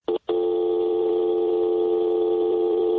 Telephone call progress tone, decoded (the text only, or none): dial tone